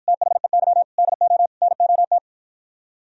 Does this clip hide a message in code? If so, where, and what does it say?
Yes: Morse code from 0.1 to 2.2 s, heard as a keyed tone, about 57 wpm, 700 Hz. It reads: THEY DO NOT